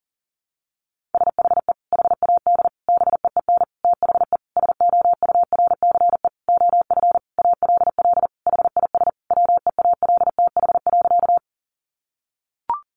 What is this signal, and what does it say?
Morse code, heard as a keyed tone, about 40 wpm, 700 Hz. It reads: SHE HAD BEEN THE SOURCE OF ALL HIS WEALTH.
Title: Sentences from Top 1200 Words 40wpm